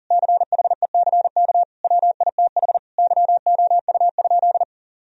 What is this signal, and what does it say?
Morse code, heard as a keyed tone, about 37 wpm, 700 Hz. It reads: CHECK WITH YOU?
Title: Sentences from Top 1000 Words - Review Part 3 - 40wpm